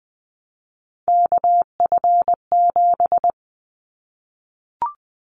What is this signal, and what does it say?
Morse code, heard as a keyed tone, about 20 wpm, 700 Hz. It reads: KF7
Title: Callsign Prefixes for Top 25 DXCC - 20wpm